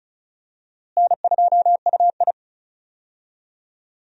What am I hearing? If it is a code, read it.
Morse code, heard as a keyed tone, about 35 wpm, 700 Hz. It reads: N2UI
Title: USA Call Signs - 4 Characters - Repeated 3x - 35wpm